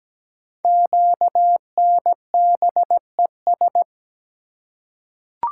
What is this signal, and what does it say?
Morse code, heard as a keyed tone, about 17 wpm, 700 Hz. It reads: QNBES
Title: Code Groups - 5 Letters - 17wpm